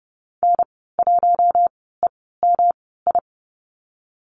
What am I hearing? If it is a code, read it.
Morse code, heard as a keyed tone, about 30 wpm, 700 Hz. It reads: N 1 E M I
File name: Lesson 19g - Letter w - Call Signs 30wpm